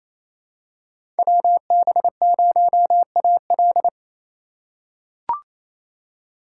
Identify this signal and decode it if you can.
Morse code, heard as a keyed tone, about 28 wpm, 700 Hz. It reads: WB0AL
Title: USA Call Signs - 3x Character Spacing - 28wpm